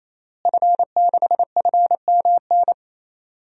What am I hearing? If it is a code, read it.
Morse code, heard as a keyed tone, about 28 wpm, 700 Hz. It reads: F6FMN
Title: DXCC-100 Call Signs - 5 Characters - Repeated 3x - 28wpm